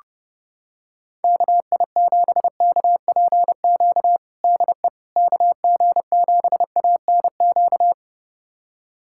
Morse code, heard as a keyed tone, about 29 wpm, 700 Hz. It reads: KI7KPQ DE KG7ANQ
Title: USA Callsign DE Callsign - 6 Characters - 30wpm